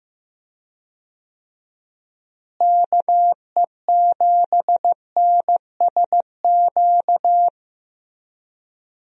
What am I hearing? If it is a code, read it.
Morse code, heard as a keyed tone, about 15 wpm, 700 Hz. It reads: KE7NSQ
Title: USA Call Signs - 6 Characters - Repeated 3x - 15wpm